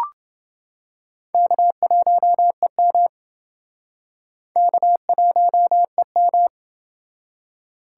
Morse code, heard as a keyed tone, about 28 wpm, 700 Hz. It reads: K1EM K1EM